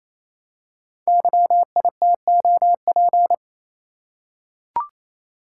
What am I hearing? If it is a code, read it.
Morse code, heard as a keyed tone, about 28 wpm, 700 Hz. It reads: YITOP
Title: Code Groups - 5 Letters - 28wpm